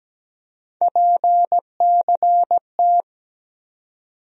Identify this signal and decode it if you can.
Morse code, heard as a keyed tone, about 17 wpm, 700 Hz. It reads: PCT
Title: USA Call Sign Postfixes - 3 Letters 17wpm